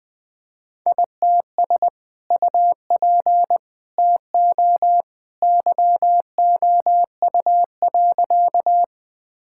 Morse code, heard as a keyed tone, about 20 wpm, 700 Hz. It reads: ITS UP TO YOU.
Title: Sentences with 2-Word Phrases - 20wpm